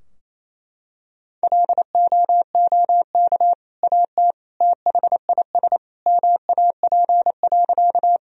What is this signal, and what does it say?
Morse code, heard as a keyed tone, about 28 wpm, 700 Hz. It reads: LOOK AT THIS MAP.
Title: Sentences from Top 1000 Words - Review Part 2 - 28wpm